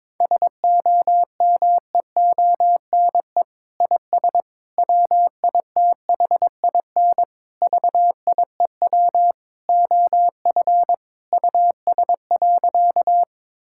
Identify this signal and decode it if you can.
Morse code, heard as a keyed tone, about 22 wpm, 700 Hz. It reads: SOMEONE IS WITHIN VIEW OF US.
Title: Sentences from Top 1000 Words - Review Part 6 - 22wpm